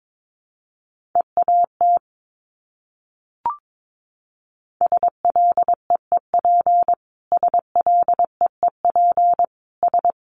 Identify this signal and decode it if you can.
Morse code, heard as a keyed tone, about 22 wpm, 700 Hz. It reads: EAT SLEEP SLEEP S
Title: Top 500 Words Repeated 22wpm